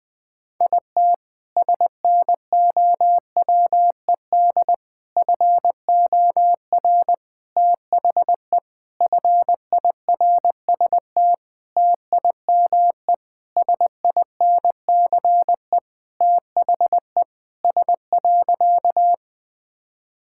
Morse code, heard as a keyed tone, about 20 wpm, 700 Hz. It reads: IT SNOWED FOR THE FIRST TIME SINCE THE S.
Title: Sentences with 6-Word Phrases - 20wpm